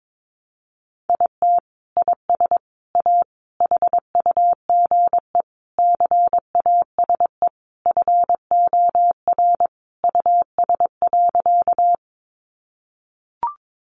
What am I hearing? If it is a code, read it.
Morse code, heard as a keyed tone, about 22 wpm, 700 Hz. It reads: IT IS A HUGE CASE FOR US.
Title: Sentences from Top 1100 Words 22wpm